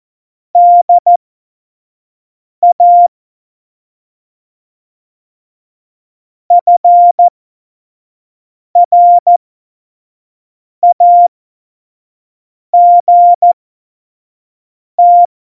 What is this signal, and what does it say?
Morse code, heard as a keyed tone, about 14 wpm, spaced as for 5 wpm, 700 Hz. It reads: DA FRAGT
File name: Morsekurs / Witze / Witz 19. Februar 2019, 25 F70